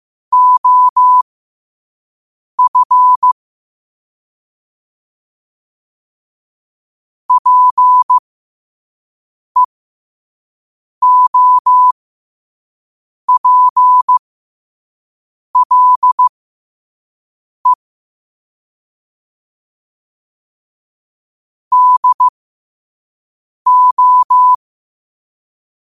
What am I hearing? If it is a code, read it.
Morse code, heard as a keyed tone, about 15 wpm, spaced as for 5 wpm, 1000 Hz. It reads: OF PEOPLE DO
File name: qotdfiles / 5wpm